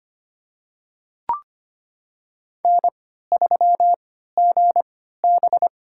Morse code, heard as a keyed tone, about 25 wpm, spaced as for 14 wpm, 700 Hz. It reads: N3GB
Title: USA Call Signs - 4 Characters - 3x Character Spacing - 25wpm